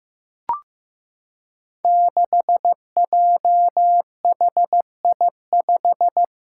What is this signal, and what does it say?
Morse code, heard as a keyed tone, about 16 wpm, 700 Hz. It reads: 6JHI5
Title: Five Challenging Letter-Number Combinations 15wpm